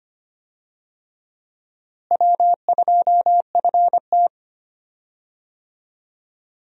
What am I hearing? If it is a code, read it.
Morse code, heard as a keyed tone, about 25 wpm, 700 Hz. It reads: W2FT
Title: USA Call Signs - 4 Characters - Repeated 3x - 25wpm